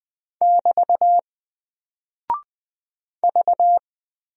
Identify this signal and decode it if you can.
Morse code, heard as a keyed tone, about 20 wpm, 700 Hz. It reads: <BT> V
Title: Single Character (Letter:Number:Punctuation:Prosign) 20wpm